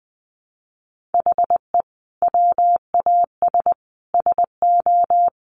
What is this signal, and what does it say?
Morse code, heard as a keyed tone, about 20 wpm, 700 Hz. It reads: HE WAS SO
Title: Sentences from Top 200 Words 20wpm